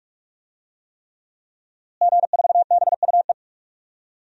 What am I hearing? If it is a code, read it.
Morse code, heard as a keyed tone, about 46 wpm, 700 Hz. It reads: G4BUE
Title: DXCC-100 Call Signs - 5 Characters - Repeated 3x - 45wpm